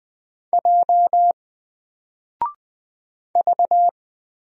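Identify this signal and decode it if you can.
Morse code, heard as a keyed tone, about 20 wpm, 700 Hz. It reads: J V
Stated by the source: Single letter and number Morse drill at 20 wpm, from N4LCD.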